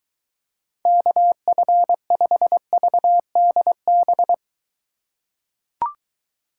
Morse code, heard as a keyed tone, about 22 wpm, 700 Hz. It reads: KF5VDB